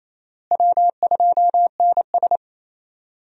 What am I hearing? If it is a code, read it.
Morse code, heard as a keyed tone, about 28 wpm, 700 Hz. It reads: W2NS
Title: USA Call Signs - 4 Characters - Repeated 2x - 28wpm